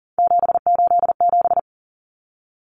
Morse code, heard as a keyed tone, about 40 wpm, 700 Hz. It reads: 787
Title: Common Cut Numbers - 3-Digit 40wpm